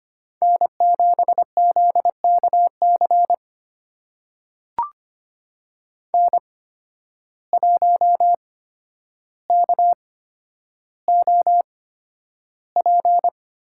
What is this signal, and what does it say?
Morse code, heard as a keyed tone, about 25 wpm, spaced as for 11 wpm, 700 Hz. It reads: N7ZKC N1KOP
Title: USA Call Signs - 5 Characters - 8x Character Spacing - 25wpm